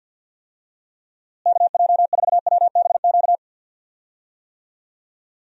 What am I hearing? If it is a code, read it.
Morse code, heard as a keyed tone, about 51 wpm, 700 Hz. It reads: KJ4WBQ
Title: USA Call Signs - 6 Characters - Repeated 2x - 50wpm